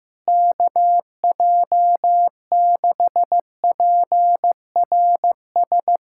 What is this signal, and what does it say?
Morse code, heard as a keyed tone, about 15 wpm, 700 Hz. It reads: KJ6PRS
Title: USA Callsign DE Callsign 15wpm